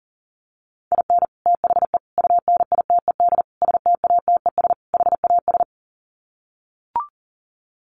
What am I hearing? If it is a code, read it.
Morse code, heard as a keyed tone, about 40 wpm, 700 Hz. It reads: IN THE UNITED STATES HAS
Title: Sets of 5 Words Encore2 40wpm